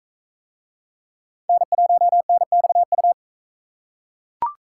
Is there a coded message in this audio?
Yes: Morse code from 1.5 to 3.1 s, heard as a keyed tone, about 38 wpm, 700 Hz. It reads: N1NXU